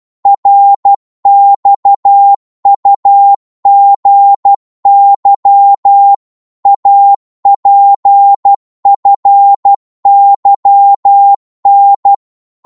Morse code, heard as a keyed tone, about 12 wpm, 800 Hz. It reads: RXUGY APFYN
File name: Morse / ltr-12wpm